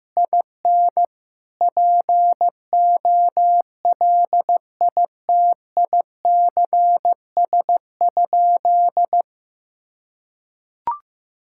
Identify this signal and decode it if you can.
Morse code, heard as a keyed tone, about 15 wpm, 700 Hz. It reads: IN POLITICS?
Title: Sentences from Top 1000 Words - Review Part 6 - 15wpm